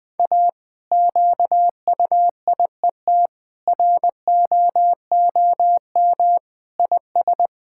Morse code, heard as a keyed tone, about 20 wpm, 700 Hz. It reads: A QUIET ROOM IS
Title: Difficult Sentences with Hints - 20wpm